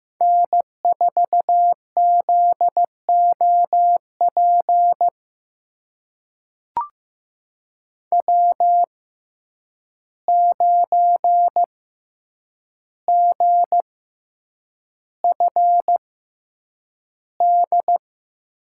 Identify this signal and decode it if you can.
Morse code, heard as a keyed tone, about 15 wpm, spaced as for 8 wpm, 700 Hz. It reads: N4ZOP W9GFD
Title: USA Call Signs - 6x Character Spacing - 15wpm